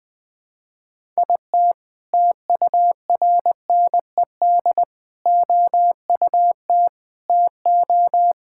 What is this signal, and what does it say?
Morse code, heard as a keyed tone, about 20 wpm, 700 Hz. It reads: IT TURNED OUT TO
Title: Sentences with 4-Word Phrases - 20wpm